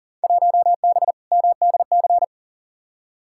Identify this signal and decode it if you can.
Morse code, heard as a keyed tone, about 40 wpm, 700 Hz. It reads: 1B MDC
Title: Field Day Exchanges Encore - 40wpm